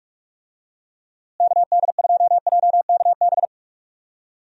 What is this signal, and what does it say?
Morse code, heard as a keyed tone, about 47 wpm, 700 Hz. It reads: KD2JKB
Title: USA Call Signs - 6 Characters - Repeated 2x - 45wpm